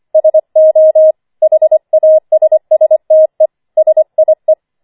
Morse code, heard as a keyed tone, about 25 wpm, 630 Hz. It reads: SO HASSTE SIE
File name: 25wpm